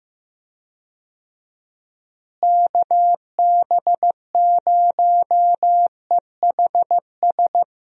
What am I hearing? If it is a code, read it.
Morse code, heard as a keyed tone, about 16 wpm, 700 Hz. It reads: KB0EHS
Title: USA Call Signs - 6 Characters - Repeated 2x - 15wpm